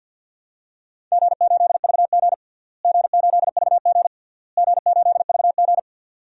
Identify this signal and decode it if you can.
Morse code, heard as a keyed tone, about 49 wpm, 700 Hz. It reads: G8VG G8VG G8VG
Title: DXCC-100 Call Signs - 4 Characters - Repeated 3x - 50wpm